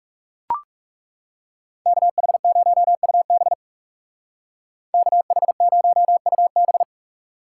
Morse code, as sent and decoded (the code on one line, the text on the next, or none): -.- .... ----- ..- -... / -.- .... ----- ..- -...
KH0UB KH0UB